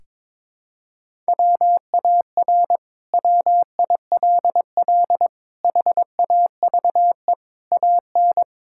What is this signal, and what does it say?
Morse code, heard as a keyed tone, about 22 wpm, 700 Hz. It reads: WAR WILL HAVE AN
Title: Sentences from Top 1000 Words - Review Part 6 - 22wpm